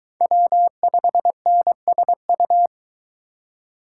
Morse code, heard as a keyed tone, about 20 wpm, 700 Hz. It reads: W5NSU